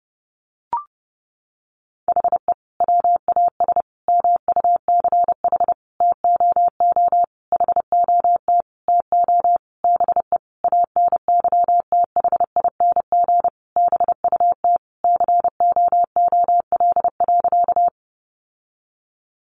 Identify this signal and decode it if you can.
Morse code, heard as a keyed tone, about 30 wpm, 700 Hz. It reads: HE WAS MUCH TOO HOT TO BE ANYTHING BUT COOL.
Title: Sentences from Top 1200 Words 30wpm